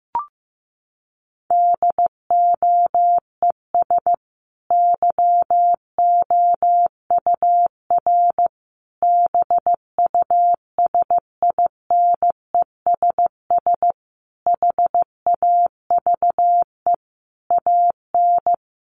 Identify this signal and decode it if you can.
Morse code, heard as a keyed tone, about 15 wpm, 700 Hz. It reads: DOES YOUR BUSINESS HAVE AN
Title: Sentences from Top 1200 Words 15wpm